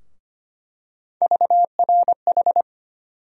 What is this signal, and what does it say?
Morse code, heard as a keyed tone, about 25 wpm, 700 Hz. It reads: VRH
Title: USA Call Sign Postfixes - 3 Letters 25wpm